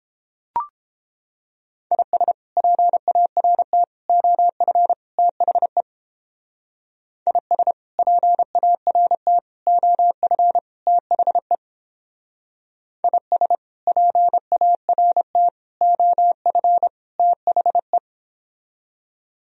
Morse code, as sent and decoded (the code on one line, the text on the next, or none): .. ... / .--. .- .-. - / --- ..-. / - .... / .. ... / .--. .- .-. - / --- ..-. / - .... / .. ... / .--. .- .-. - / --- ..-. / - .... .
IS PART OF THE IS PART OF THE IS PART OF THE